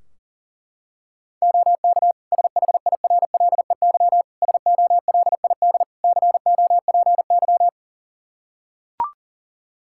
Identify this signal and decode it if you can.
Morse code, heard as a keyed tone, about 37 wpm, 700 Hz. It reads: OK SHIRLEY SOLID COPY
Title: Practice QSO - Exchange 3 - More Info - 40wpm